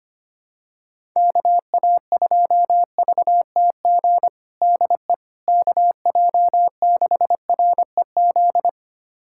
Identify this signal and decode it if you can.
Morse code, heard as a keyed tone, about 25 wpm, 700 Hz. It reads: KA2VTG DE KJ6REZ
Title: USA Callsign DE Callsign - 6 Characters - 25wpm